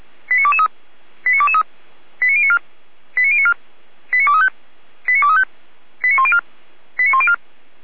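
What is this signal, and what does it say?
Five-tone selective call (ZVEI), heard as quick runs of five tones, tones of about 70 ms. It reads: 89223 89223 89094 89094 89235 89235 89104 89104
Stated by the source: ZVEI five-tone selective call received by SDR.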